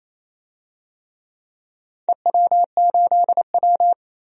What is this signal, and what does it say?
Morse code, heard as a keyed tone, about 28 wpm, 700 Hz. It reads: EW8W
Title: DXCC-100 Call Signs - 4 Characters - 28wpm